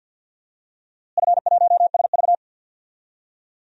Morse code, heard as a keyed tone, about 52 wpm, 700 Hz. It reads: F1SV
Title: DXCC-100 Call Signs - 4 Characters - Repeated 3x - 50wpm